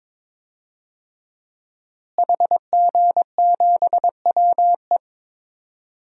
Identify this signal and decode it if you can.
Morse code, heard as a keyed tone, about 22 wpm, 700 Hz. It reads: HG7WE
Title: DXCC-100 Call Signs - 5 Characters - Repeated 2x - 22wpm